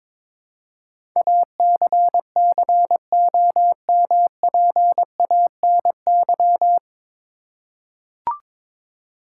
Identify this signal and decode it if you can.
Morse code, heard as a keyed tone, about 22 wpm, 700 Hz. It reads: ACCOMPANY
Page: Morse Code Ninja practice 22wpm